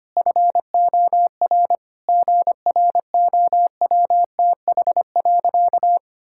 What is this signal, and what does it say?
Morse code, heard as a keyed tone, about 25 wpm, 700 Hz. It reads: FOR GROWTH.
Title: Sentences from Top 1000 Words - Review Part 5 - 25wpm